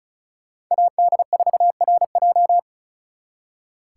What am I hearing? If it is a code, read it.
Morse code, heard as a keyed tone, about 43 wpm, 700 Hz. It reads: AD4RJ
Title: Speed Racing - USA 2x2 Call Signs 35wpm